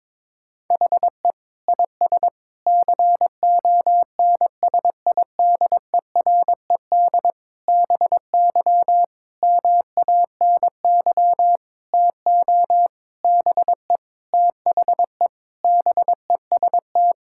Morse code, heard as a keyed tone, about 22 wpm, 700 Hz. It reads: HE IS CONSIDERED BY MANY TO BE THE BEST
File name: Sentences with 7-Word Phrases - 22wpm